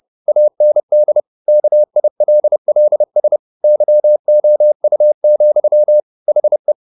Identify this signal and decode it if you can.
Morse code, heard as a keyed tone, about 30 wpm, 600 Hz. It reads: AND KILLS YOU, HE
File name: the old man and the sea / the old man and the sea 1 30wpm